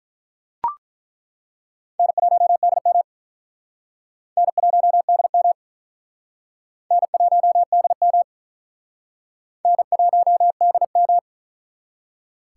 Morse code, heard as a keyed tone, about 48 wpm, 700 Hz. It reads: N1DM N1DM N1DM N1DM